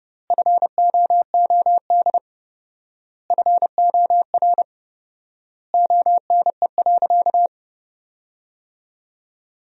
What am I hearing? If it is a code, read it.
Morse code, heard as a keyed tone, about 30 wpm, 700 Hz. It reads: FOOD FOR ONE.
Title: 4x Word-Spacing - Sentences from Top 300 Words - 30wpm